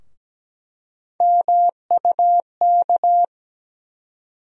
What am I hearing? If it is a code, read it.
Morse code, heard as a keyed tone, about 17 wpm, 700 Hz. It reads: MUK
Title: USA Call Sign Postfixes - 3 Letters 17wpm